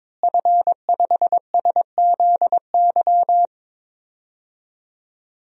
Morse code, heard as a keyed tone, about 22 wpm, 700 Hz. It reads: F5SZY
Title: DXCC-100 Call Signs - 5 Characters - Repeated 3x - 22wpm